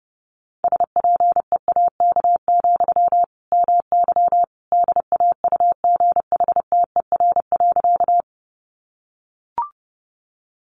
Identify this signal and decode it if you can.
Morse code, heard as a keyed tone, about 28 wpm, 700 Hz. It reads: SPEAK, MY DAUGHTER.
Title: Sentences from Top 1500 Words 30wpm